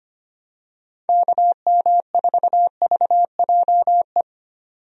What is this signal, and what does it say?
Morse code, heard as a keyed tone, about 25 wpm, 700 Hz. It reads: KM4VJE
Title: USA Call Signs - 6 Characters - 25wpm